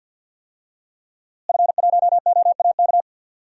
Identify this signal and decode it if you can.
Morse code, heard as a keyed tone, about 50 wpm, 700 Hz. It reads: F1OAK
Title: DXCC-100 Call Signs - 5 Characters - Repeated 3x - 50wpm